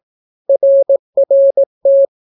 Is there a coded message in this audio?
Yes: Morse code from 0.5 to 2.0 s, heard as a keyed tone, about 17 wpm, 550 Hz. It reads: RRT